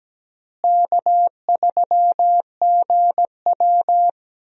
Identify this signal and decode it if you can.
Morse code, heard as a keyed tone, about 17 wpm, 700 Hz. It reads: K3GW
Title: USA Call Signs - 4 Characters - Repeated 3x - 17wpm